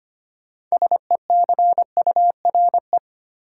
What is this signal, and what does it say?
Morse code, heard as a keyed tone, about 25 wpm, 700 Hz. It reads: SECURE
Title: Sentences from Top 1200 Words - New Word Drill Alt - 25wpm